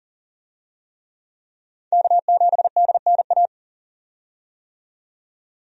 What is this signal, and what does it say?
Morse code, heard as a keyed tone, about 40 wpm, 700 Hz. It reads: K7DNA
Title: USA Call Signs - 5 Characters - 40wpm